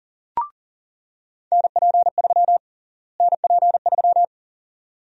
Morse code, heard as a keyed tone, about 40 wpm, 700 Hz. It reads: NP3 NP3
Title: USA Call Sign Prefixes - Repeated 2x - 40wpm